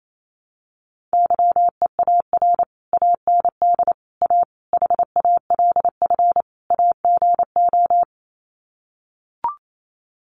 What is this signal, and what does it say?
Morse code, heard as a keyed tone, about 28 wpm, 700 Hz. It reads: YEAR AND A HALF AGO
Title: Sets of 5 Words Encore 28wpm